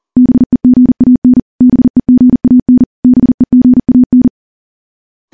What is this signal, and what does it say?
Morse code, heard as a keyed tone, about 43 wpm, 250 Hz. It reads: BEGAN BEGAN BEGAN
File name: morse / morse-training-40WPM